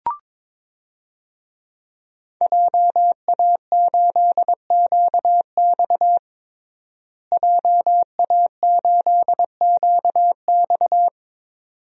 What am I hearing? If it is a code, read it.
Morse code, heard as a keyed tone, about 22 wpm, 700 Hz. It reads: JA8QX JA8QX
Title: DXCC-100 Call Signs - 5 Characters - Repeated 2x - 22wpm